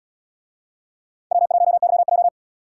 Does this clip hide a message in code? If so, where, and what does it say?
Yes: Morse code from 1.3 to 2.3 s, heard as a keyed tone, about 77 wpm, 700 Hz. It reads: W1YJ